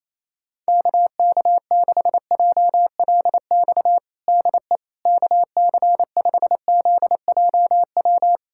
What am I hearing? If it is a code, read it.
Morse code, heard as a keyed tone, about 28 wpm, 700 Hz. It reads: KK6JLX DE KC5ZJW
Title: USA Callsign DE Callsign 28wpm